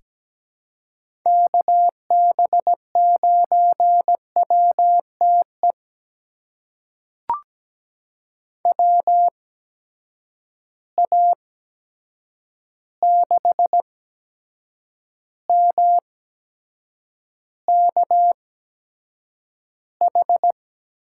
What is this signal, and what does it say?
Morse code, heard as a keyed tone, about 17 wpm, spaced as for 7 wpm, 700 Hz. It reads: KB9WTE WA6MKH
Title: USA Call Signs - 6 Characters - 8x Character Spacing - 17wpm